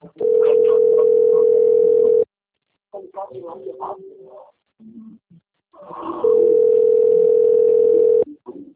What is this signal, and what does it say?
Ringback tone (North America), 440 + 480 Hz, with speech.
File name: rg-1600-unknown- - - .14955